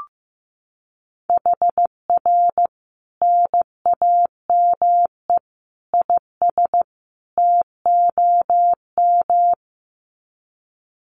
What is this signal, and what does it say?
Morse code, heard as a keyed tone, about 15 wpm, 700 Hz. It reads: HR NAME IS TOM
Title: Common QSO Phrases 15wpm